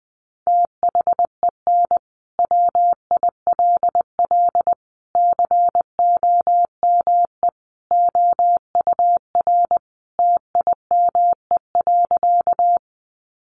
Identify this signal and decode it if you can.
Morse code, heard as a keyed tone, about 20 wpm, 700 Hz. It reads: THEN WILL COME OUR TIME.
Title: Sentences from Top 200 Words 20wpm